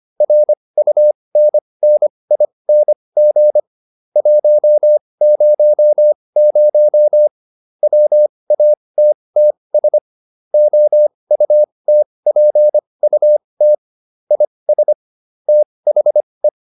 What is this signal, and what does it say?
Morse code, heard as a keyed tone, about 25 wpm, spaced as for 20 wpm, 600 Hz. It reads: RUNNING 100 WATTS OUTPUT IS THE